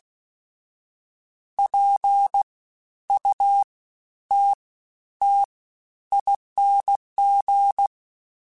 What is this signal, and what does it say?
Morse code, heard as a keyed tone, about 16 wpm, 800 Hz. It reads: P U T T ING
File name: Special INGs-mp3s / Special INGs-mp3s / SlowToFaster ing @15